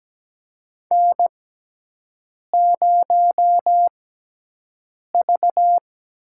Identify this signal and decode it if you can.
Morse code, heard as a keyed tone, about 17 wpm, spaced as for 6 wpm, 700 Hz. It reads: N0V
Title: USA Call Signs - 6x Character Spacing - 17wpm